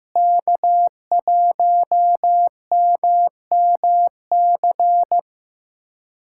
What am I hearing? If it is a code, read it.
Morse code, heard as a keyed tone, about 16 wpm, 700 Hz. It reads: K1MMC
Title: USA Callsign DE Callsign - 5 Characters - 15wpm